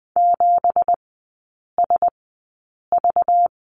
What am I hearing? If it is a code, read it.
Morse code, heard as a keyed tone, about 20 wpm, spaced as for 8 wpm, 700 Hz. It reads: 7SV
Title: 191 Lesson 31e - Letter v - Three Characters 20wpm